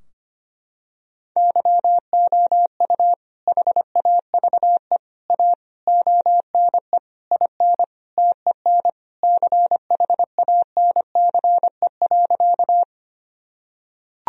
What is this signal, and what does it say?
Morse code, heard as a keyed tone, about 25 wpm, 700 Hz. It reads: YOU HAVE A ONE IN TEN CHANCE.
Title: Sentences from Top 1000 Words - Review Part 2 - 25wpm